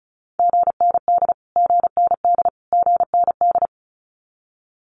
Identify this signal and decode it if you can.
Morse code, heard as a keyed tone, about 34 wpm, 700 Hz. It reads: GND GND GND
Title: Abbreviations Repeated 3x - 35wpm